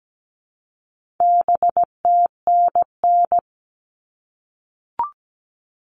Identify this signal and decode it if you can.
Morse code, heard as a keyed tone, about 17 wpm, 700 Hz. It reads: BTNN
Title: All Cut Numbers - 4-Digit 17wpm